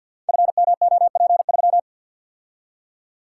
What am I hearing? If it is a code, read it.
Morse code, heard as a keyed tone, about 52 wpm, 700 Hz. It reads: FMOP3
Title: Code Groups - 5 Letter:Numbers - 50wpm